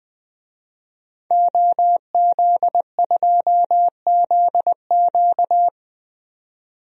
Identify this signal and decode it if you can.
Morse code, heard as a keyed tone, about 20 wpm, 700 Hz. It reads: OZ2ZQ
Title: DXCC-100 Call Signs - 5 Characters - 20wpm